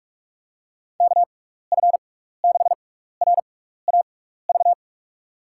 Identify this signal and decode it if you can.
Morse code, heard as a keyed tone, about 46 wpm, spaced as for 15 wpm, 700 Hz. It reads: KF6RAV
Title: USA Call Signs - 6x Character Spacing - 45wpm